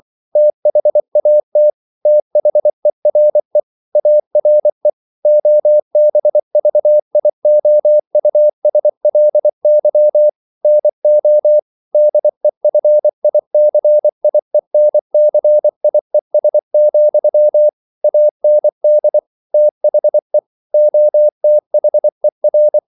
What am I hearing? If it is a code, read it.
Morse code, heard as a keyed tone, about 24 wpm, 600 Hz. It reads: THAT THERE ARE OBVIOUSLY NO DEFICIENCIES, AND THE OTHER
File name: CW / Practice / EnglishQuotations0224